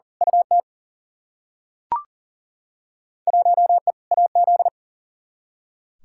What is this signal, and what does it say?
Morse code, heard as a keyed tone, about 40 wpm, 700 Hz. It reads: UT 1E AZ
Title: Field Day Exchanges - 40wpm